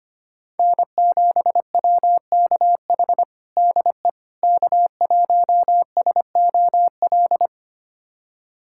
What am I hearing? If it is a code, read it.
Morse code, heard as a keyed tone, about 25 wpm, 700 Hz. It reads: N7WKH DE K1SOL
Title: USA Callsign DE Callsign - 5 Characters - 25wpm